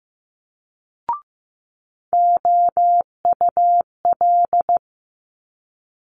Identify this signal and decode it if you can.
Morse code, heard as a keyed tone, about 15 wpm, 700 Hz. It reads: OUL
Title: Multi-letter Phonemes Quick Fire 15wpm